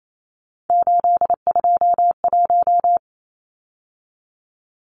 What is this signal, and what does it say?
Morse code, heard as a keyed tone, about 28 wpm, 700 Hz. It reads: 821
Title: Numbers - 3-Digit 28wpm